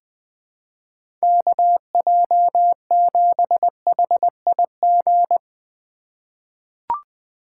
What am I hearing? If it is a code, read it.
Morse code, heard as a keyed tone, about 20 wpm, 700 Hz. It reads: KJ7HIG
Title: USA Call Signs - 6 Characters - 3x Character Spacing - 20wpm